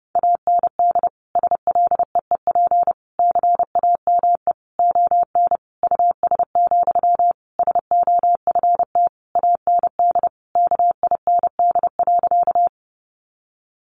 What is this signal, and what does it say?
Morse code, heard as a keyed tone, about 28 wpm, 700 Hz. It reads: AND SLEEP CAME ON US, SOFT AND KIND.
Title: Sentences from Top 1500 Words 30wpm